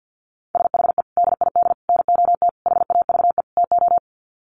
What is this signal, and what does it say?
Morse code, heard as a keyed tone, about 50 wpm, 700 Hz. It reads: SHE DID NOT HAVE TO
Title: Sets of 5 Words Encore2 50wpm